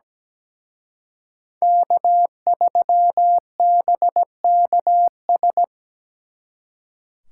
Morse code, heard as a keyed tone, about 20 wpm, 700 Hz. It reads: K3BKS
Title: Speed Racing - USA 1x3 Call Signs 17wpm